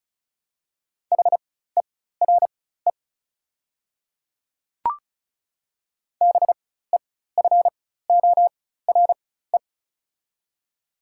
Morse code, heard as a keyed tone, about 36 wpm, spaced as for 16 wpm, 700 Hz. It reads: HERE BEFORE
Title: 4x Character Spacing - Top 300 Words - 35wpm